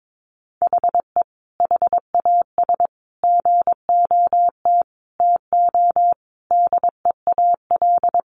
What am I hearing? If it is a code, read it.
Morse code, heard as a keyed tone, about 22 wpm, 700 Hz. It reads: HE HAS GOT TO DEAL
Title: Sentences from Top 800 Words 22wpm